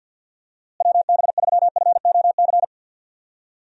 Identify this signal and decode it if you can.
Morse code, heard as a keyed tone, about 47 wpm, 700 Hz. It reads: WB3FOC
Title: USA Call Signs - 6 Characters - Repeated 2x - 50wpm